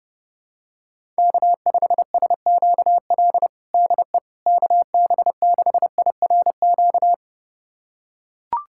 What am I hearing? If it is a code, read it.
Morse code, heard as a keyed tone, about 32 wpm, 700 Hz. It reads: K5SQL DE KB6IRQ
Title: USA Callsign DE Callsign 30wpm